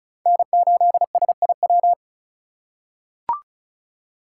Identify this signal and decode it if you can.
Morse code, heard as a keyed tone, about 34 wpm, 700 Hz. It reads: N8SIW
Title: USA Call Signs - 5 Characters - 4x Character Spacing - 35wpm